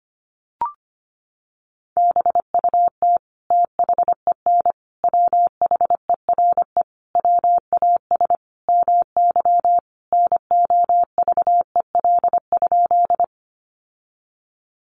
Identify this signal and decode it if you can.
Morse code, heard as a keyed tone, about 25 wpm, 700 Hz. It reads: BUT THEN WHERE WAS MY NOVEL?
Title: Sentences from Top 1300 Words 25wpm